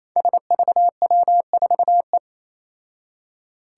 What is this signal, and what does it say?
Morse code, heard as a keyed tone, about 27 wpm, 700 Hz. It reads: SVW4E
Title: Five Challenging Letter-Number Combinations 28wpm